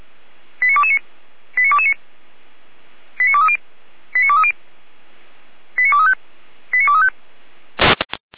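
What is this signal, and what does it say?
Five-tone selective call (ZVEI), heard as quick runs of five tones, tones of about 80 ms. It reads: 89229 89229 89230 89230 89235 89235
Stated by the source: ZVEI five-tone selective call received on an SDR.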